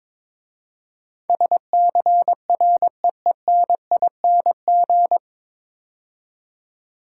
Morse code, heard as a keyed tone, about 23 wpm, 700 Hz. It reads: SCREENING
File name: 9 Letter Words - First 3 Letters Given - 22wpm